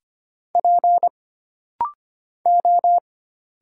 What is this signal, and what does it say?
Morse code, heard as a keyed tone, about 25 wpm, 700 Hz. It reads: P O
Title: Single Letter-Number Alt - Rapid-Fire 25wpm